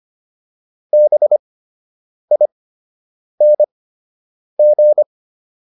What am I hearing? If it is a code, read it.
Morse code, heard as a keyed tone, about 25 wpm, spaced as for 8 wpm, 600 Hz. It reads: BING